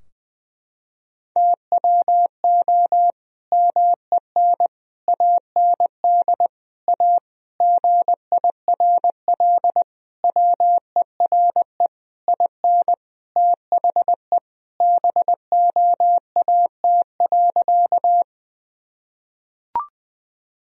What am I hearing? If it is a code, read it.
Morse code, heard as a keyed tone, about 20 wpm, 700 Hz. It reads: TWO MEN AND A GIRL WERE IN THE BOAT.
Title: Sentences from Top 1000 Words - Review Part 3 - 20wpm